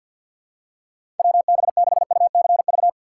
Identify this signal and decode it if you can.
Morse code, heard as a keyed tone, about 52 wpm, 700 Hz. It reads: WB6UCV
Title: USA Call Signs - 6 Characters - 4x Character Spacing - 50wpm